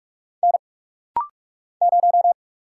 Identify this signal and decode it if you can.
Morse code, heard as a keyed tone, about 46 wpm, 700 Hz. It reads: N 0
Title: Single Letter-Number - Rapid-Fire 45wpm